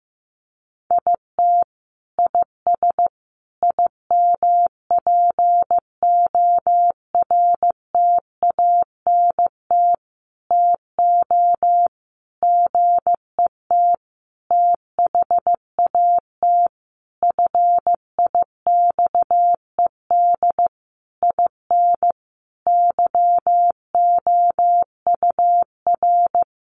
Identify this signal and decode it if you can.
Morse code, heard as a keyed tone, about 15 wpm, 700 Hz. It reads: IT IS IMPORTANT TO GET THAT FIXED IN YOUR
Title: Sentences from Top 1100 Words 15wpm